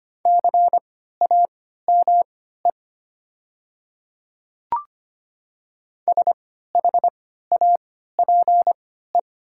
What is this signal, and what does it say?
Morse code, heard as a keyed tone, about 25 wpm, spaced as for 14 wpm, 690 Hz. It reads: CAME SHAPE
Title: 3x Character Spacing - Top 500 Words - 25wpm